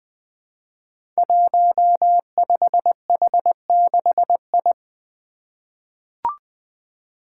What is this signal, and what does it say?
Morse code, heard as a keyed tone, about 20 wpm, 700 Hz. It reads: 15H6I
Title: Five Challenging Letter-Number Combinations 20wpm